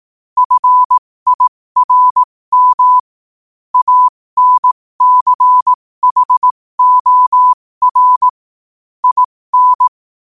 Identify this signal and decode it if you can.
Morse code, heard as a keyed tone, about 18 wpm, spaced as for 15 wpm, 1000 Hz. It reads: FIRM ANCHOR IN